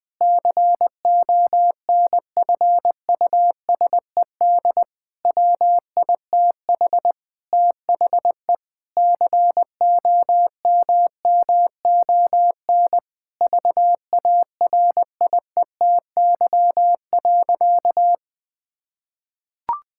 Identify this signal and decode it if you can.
Morse code, heard as a keyed tone, about 21 wpm, 700 Hz. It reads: CONFUSED WITH THE COMMON VARIETY.
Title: Sentences with 6-Word Phrases - 20wpm